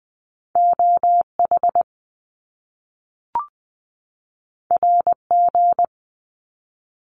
Morse code, heard as a keyed tone, about 20 wpm, 700 Hz. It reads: O H R G
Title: Top Two Letter Combinations in Words 20wpm